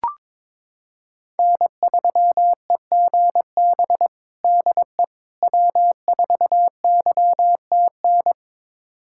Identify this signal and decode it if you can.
Morse code, heard as a keyed tone, about 22 wpm, 700 Hz. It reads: N3EGB DE W4YTN
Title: USA Callsign DE Callsign - 5 Characters - 22wpm